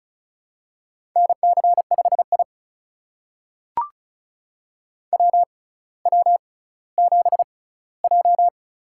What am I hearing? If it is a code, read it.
Morse code, heard as a keyed tone, about 36 wpm, spaced as for 18 wpm, 700 Hz. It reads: NC5I WW7J
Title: USA Call Signs - 4 Characters - 6x Character Spacing - 35wpm